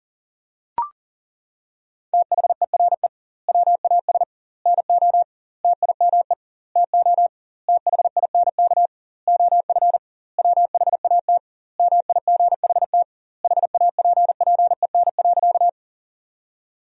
Morse code, heard as a keyed tone, about 40 wpm, 700 Hz. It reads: THERE WAS NO TIME TO THINK OF WHAT MIGHT HAPPEN.
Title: 2x Word-Spacing - Sentences from Top 400 Words - 40wpm